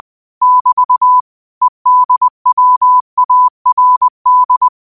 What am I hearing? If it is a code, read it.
Morse code, heard as a keyed tone, about 20 wpm, 1000 Hz. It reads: <BT> EDWARD